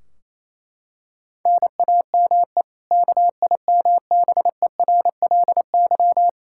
Morse code, heard as a keyed tone, about 28 wpm, 700 Hz. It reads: NAME KIMBERLY
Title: Practice QSO - Exchange 1 - Beginning - 28wpm